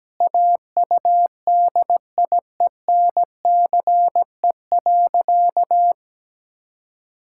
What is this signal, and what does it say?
Morse code, heard as a keyed tone, about 17 wpm, 700 Hz. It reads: AUDIENCE.
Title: Sentences from Top 1000 Words - Review Part 5 - 17wpm